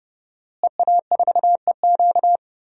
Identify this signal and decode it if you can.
Morse code, heard as a keyed tone, about 30 wpm, 700 Hz. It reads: EA4EQ
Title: DXCC-100 Call Signs - 5 Characters - Repeated 3x - 30wpm